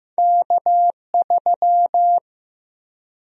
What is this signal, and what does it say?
Morse code, heard as a keyed tone, about 15 wpm, 700 Hz. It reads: K3
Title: USA Call Sign Prefixes - Repeated 2x - 15wpm